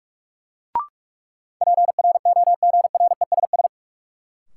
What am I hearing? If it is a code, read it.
Morse code, heard as a keyed tone, about 42 wpm, 700 Hz. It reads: PROGRESS
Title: Sentences from Top 1400 Words - New Word Drill Alt - 45wpm